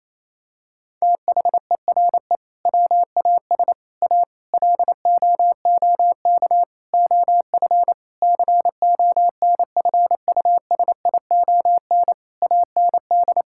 Morse code, heard as a keyed tone, about 28 wpm, 700 Hz. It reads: THERE WAS A LOOK OF CONFUSION AND
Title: Sentences from Top 1600 Words 28wpm